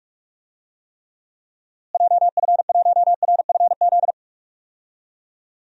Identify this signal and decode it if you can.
Morse code, heard as a keyed tone, about 46 wpm, 700 Hz. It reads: JF1RFZ
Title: DXCC-100 Call Signs - 6 Characters - Repeated 2x - 45wpm